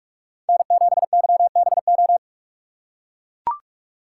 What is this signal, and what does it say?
Morse code, heard as a keyed tone, about 41 wpm, 700 Hz. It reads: N7YBO
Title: USA Call Signs - 5 Characters - 2x Character Spacing - 45wpm